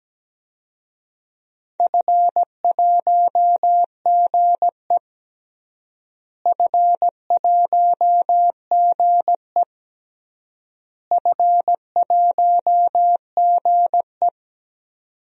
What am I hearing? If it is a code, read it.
Morse code, heard as a keyed tone, about 17 wpm, 700 Hz. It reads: F1GE F1GE F1GE
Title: DXCC-100 Call Signs - 4 Characters - Repeated 3x - 17wpm